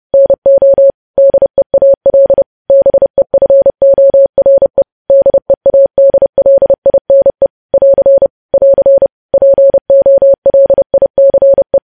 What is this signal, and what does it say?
Morse code, heard as a keyed tone, about 30 wpm, 550 Hz. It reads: NO DEAL BEFORE DEADLINE <AR> <AR> POLICE